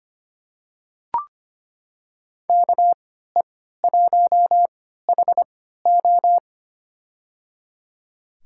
Morse code, heard as a keyed tone, about 25 wpm, spaced as for 14 wpm, 700 Hz. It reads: KE1HO
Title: USA Call Signs - 5 Characters - 3x Character Spacing - 25wpm